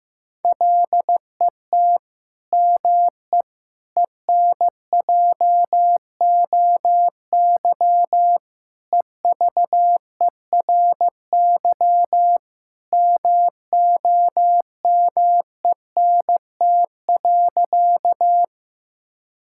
Morse code, heard as a keyed tone, about 15 wpm, 700 Hz. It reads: LET ME ENJOY EVERY MOMENT.
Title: Sentences from Top 1000 Words - Review Part 5 - 15wpm